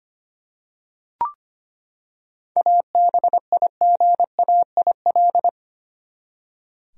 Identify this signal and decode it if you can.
Morse code, heard as a keyed tone, about 25 wpm, 700 Hz. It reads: ABIGAIL
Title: USA Common Names - 25wpm